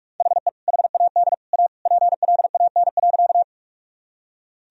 Morse code, heard as a keyed tone, about 42 wpm, 700 Hz. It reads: HE HAD A PLAN.
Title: Sentences from Top 1000 Words - Review Part 2 - 45wpm